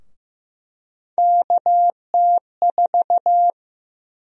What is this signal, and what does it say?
Morse code, heard as a keyed tone, about 15 wpm, 700 Hz. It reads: KT4
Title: USA Call Sign Prefixes - 15wpm